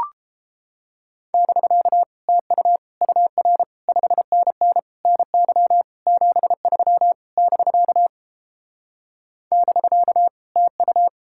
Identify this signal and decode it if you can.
Morse code, heard as a keyed tone, about 31 wpm, 700 Hz. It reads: <BK> TU UR 5NN NY 73 <BK> <BK> TU